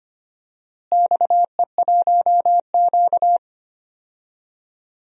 Morse code, heard as a keyed tone, about 25 wpm, 700 Hz. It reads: XE1Q
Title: DXCC-100 Call Signs - 4 Characters - Repeated 2x - 25wpm